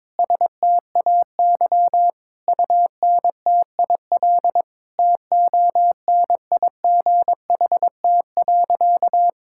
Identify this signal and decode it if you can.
Morse code, heard as a keyed tone, about 22 wpm, 700 Hz. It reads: STAY UNTIL TONIGHT.
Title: Sentences from Top 1000 Words - Review Part 6 - 22wpm